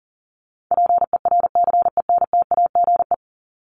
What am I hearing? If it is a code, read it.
Morse code, heard as a keyed tone, about 40 wpm, 700 Hz. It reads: PERCENTAGE
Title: Top 1001-1250 Words 40wpm